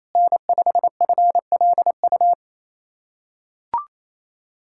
Morse code, heard as a keyed tone, about 28 wpm, 700 Hz. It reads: N5FLU
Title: USA Callsign DE Callsign - 5 Characters - 28wpm